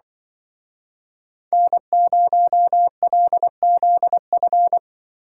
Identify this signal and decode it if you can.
Morse code, heard as a keyed tone, about 24 wpm, 700 Hz. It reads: N0LZF
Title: Speed Racing - USA 1x3 Call Signs 20wpm